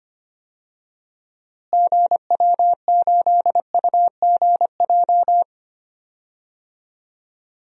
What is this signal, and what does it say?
Morse code, heard as a keyed tone, about 25 wpm, 700 Hz. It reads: GW8UGJ
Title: DXCC-100 Call Signs - 6 Characters - Repeated 3x - 25wpm